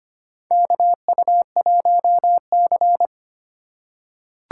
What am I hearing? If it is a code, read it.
Morse code, heard as a keyed tone, about 25 wpm, 700 Hz. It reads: KU1C
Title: USA Call Signs - 4 Characters - 2x Character Spacing - 25wpm